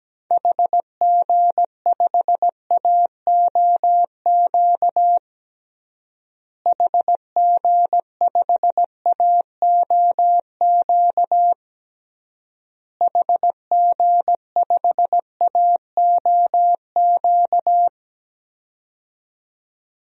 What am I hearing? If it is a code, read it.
Morse code, heard as a keyed tone, about 17 wpm, 700 Hz. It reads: HG5AOQ HG5AOQ HG5AOQ
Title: DXCC-100 Call Signs - 6 Characters - Repeated 3x - 17wpm